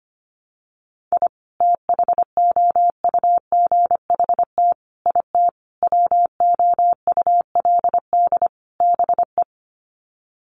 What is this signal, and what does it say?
Morse code, heard as a keyed tone, about 25 wpm, 700 Hz. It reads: I THOUGHT IT WOULD BE
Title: Sets of 5 Words Encore 25wpm